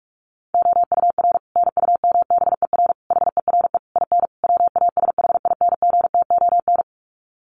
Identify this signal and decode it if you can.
Morse code, heard as a keyed tone, about 44 wpm, 700 Hz. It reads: OUR NUMBER HERE IN WASHINGTON
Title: Sets of 5 Words Encore2 45wpm